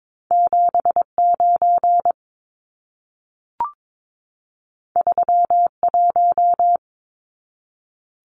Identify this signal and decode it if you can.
Morse code, heard as a keyed tone, about 22 wpm, 700 Hz. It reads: 79 31
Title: Numbers - 2-Digit 22wpm